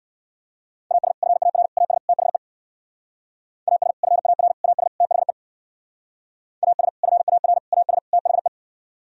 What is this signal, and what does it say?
Morse code, heard as a keyed tone, about 64 wpm, 700 Hz. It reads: AS FAR AS THE AS FAR AS THE AS FAR AS THE